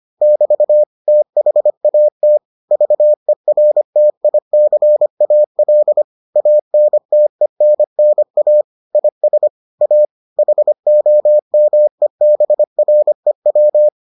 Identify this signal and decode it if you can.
Morse code, heard as a keyed tone, about 25 wpm, 600 Hz. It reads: <BT> THAT VERTICAL ANTENNA IS A HOMEBREW